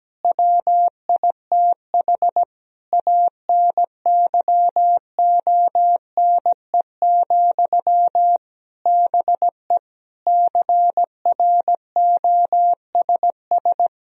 Morse code, heard as a keyed tone, about 17 wpm, 700 Hz. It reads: WITH ANYONE, BE CROSS
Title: Sentences from Top 1000 Words - Review Part 1 - 17wpm